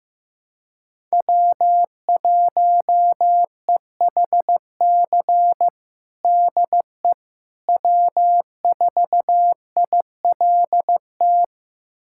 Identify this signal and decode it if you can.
Morse code, heard as a keyed tone, about 15 wpm, 700 Hz. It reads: W1EHC DE W4ILT
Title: USA Callsign DE Callsign - 5 Characters - 15wpm